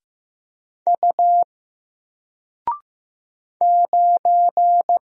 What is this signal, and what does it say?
Morse code, heard as a keyed tone, about 15 wpm, 700 Hz. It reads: U 9
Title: Single Letter-Number Alt 15wpm